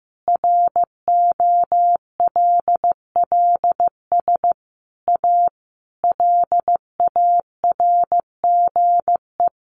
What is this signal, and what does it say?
Morse code, heard as a keyed tone, about 15 wpm, 700 Hz. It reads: ROLLS A LARGE
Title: Easy Phrases 15wpm